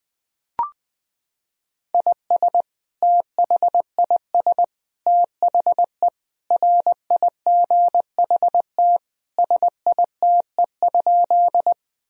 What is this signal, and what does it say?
Morse code, heard as a keyed tone, about 20 wpm, 700 Hz. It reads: IS THIS THE RIGHT SITE?
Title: Sentences from Top 1000 Words - Review Part 4 - 20wpm